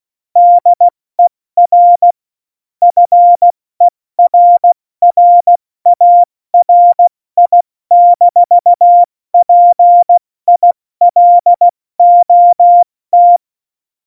Morse code, heard as a keyed tone, about 16 wpm, 700 Hz. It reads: DER FERRARI-PILOT
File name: Morsekurs / Witze / Witz 04 August 2020, 70 F80